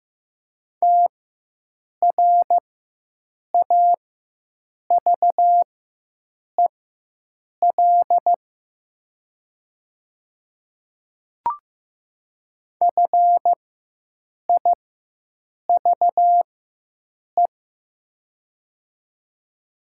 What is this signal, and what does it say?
Morse code, heard as a keyed tone, about 15 wpm, spaced as for 7 wpm, 700 Hz. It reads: TRAVEL FIVE
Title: 4x Character Spacing - Top 500 Words - 15wpm